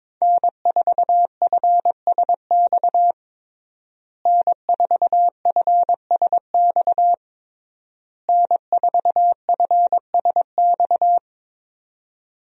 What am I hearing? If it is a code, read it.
Morse code, heard as a keyed tone, about 22 wpm, 700 Hz. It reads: N4FSX N4FSX N4FSX
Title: USA Call Signs - 5 Characters - Repeated 3x - 22wpm